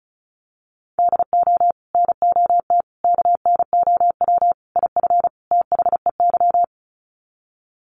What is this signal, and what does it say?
Morse code, heard as a keyed tone, about 34 wpm, 700 Hz. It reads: DO NOT KNOW IF THEY
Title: Sets of 5 Words Encore 35wpm